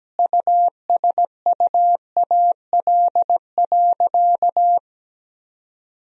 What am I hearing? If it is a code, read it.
Morse code, heard as a keyed tone, about 17 wpm, 700 Hz. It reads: USUAL.
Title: Sentences from Top 1000 Words - Review Part 2 - 17wpm